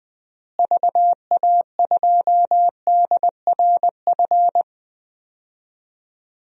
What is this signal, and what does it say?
Morse code, heard as a keyed tone, about 21 wpm, 700 Hz. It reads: VA2DRF
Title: DXCC-100 Call Signs - 6 Characters - Repeated 3x - 20wpm